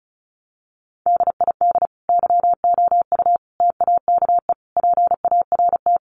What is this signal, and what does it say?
Morse code, heard as a keyed tone, about 34 wpm, 700 Hz. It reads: DID YOU TAKE PART
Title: Sentences from Top 1200 Words 35wpm